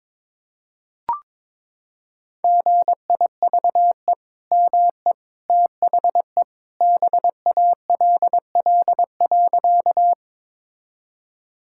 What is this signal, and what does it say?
Morse code, heard as a keyed tone, about 22 wpm, 700 Hz. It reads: GIVE ME THE BALL.
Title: Sentences from Top 1000 Words - Review Part 6 - 22wpm